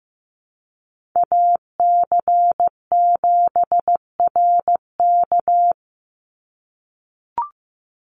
Morse code, heard as a keyed tone, about 15 wpm, 700 Hz. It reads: AC7RK
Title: USA Call Signs 15wpm